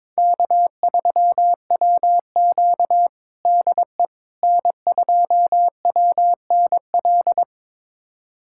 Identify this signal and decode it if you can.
Morse code, heard as a keyed tone, about 21 wpm, 700 Hz. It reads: K3WQ DE N2WNL
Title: USA Callsign DE Callsign 22wpm